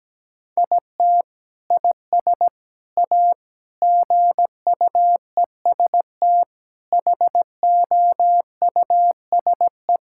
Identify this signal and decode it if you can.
Morse code, heard as a keyed tone, about 17 wpm, 700 Hz. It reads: IT IS A GUEST HOUSE
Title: Sentences from Top 1600 Words 17wpm